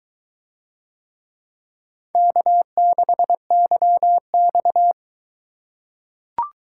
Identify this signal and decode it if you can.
Morse code, heard as a keyed tone, about 21 wpm, 700 Hz. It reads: K6YX